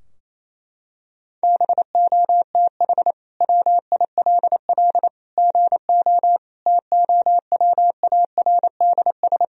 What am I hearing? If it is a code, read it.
Morse code, heard as a keyed tone, about 28 wpm, 700 Hz. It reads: BOTH WILL GO TOWARDS
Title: Sentences from Top 1000 Words - Review Part 2 - 28wpm